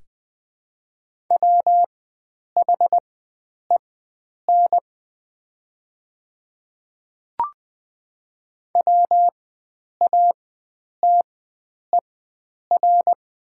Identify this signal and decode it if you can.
Morse code, heard as a keyed tone, about 20 wpm, spaced as for 9 wpm, 700 Hz. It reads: WHEN WATER
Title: 4x Character Spacing - Top 100 Words - 20wpm